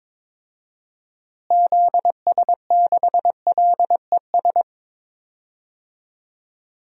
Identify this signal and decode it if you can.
Morse code, heard as a keyed tone, about 22 wpm, 700 Hz. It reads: ZS6LES
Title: DXCC-100 Call Signs - 6 Characters - Repeated 3x - 22wpm